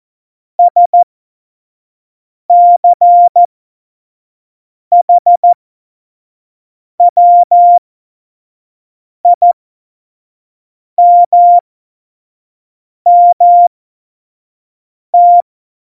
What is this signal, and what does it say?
Morse code, heard as a keyed tone, about 14 wpm, spaced as for 5 wpm, 700 Hz. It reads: SCHWIMMT
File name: Morsekurs / Witze / Witz 10. September 2019, 25 F70